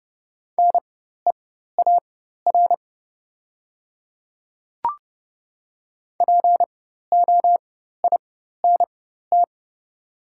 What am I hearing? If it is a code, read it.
Morse code, heard as a keyed tone, about 30 wpm, spaced as for 14 wpm, 700 Hz. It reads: NEAR POINT
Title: 4x Character Spacing - Top 300 Words - 30wpm